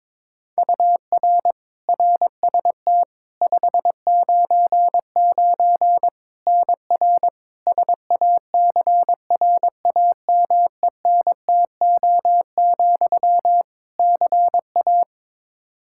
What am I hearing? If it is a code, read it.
Morse code, heard as a keyed tone, about 22 wpm, 700 Hz. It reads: UR RST 599 NR SACRAMENTO, CA
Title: Practice QSO - Exchange 1 - Beginning - 22wpm